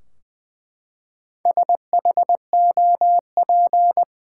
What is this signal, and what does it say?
Morse code, heard as a keyed tone, about 20 wpm, 700 Hz. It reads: SHOP
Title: Four Letter Words - 20wpm